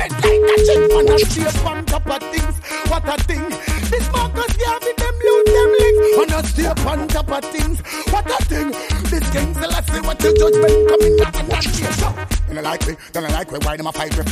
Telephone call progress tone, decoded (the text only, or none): ringback tone